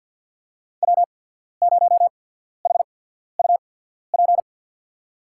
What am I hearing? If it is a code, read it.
Morse code, heard as a keyed tone, about 53 wpm, spaced as for 14 wpm, 700 Hz. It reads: W0HUP
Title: USA Call Signs - 5 Characters - 8x Character Spacing - 50wpm